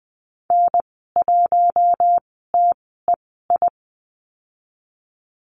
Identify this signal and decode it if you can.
Morse code, heard as a keyed tone, about 20 wpm, spaced as for 14 wpm, 700 Hz. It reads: N1TEI
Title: 042 Lesson 8g - Number 1 - Call Signs 20wpm